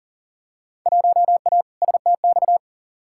Morse code, heard as a keyed tone, about 40 wpm, 700 Hz. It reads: 1A STX
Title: Field Day Exchanges Encore - 40wpm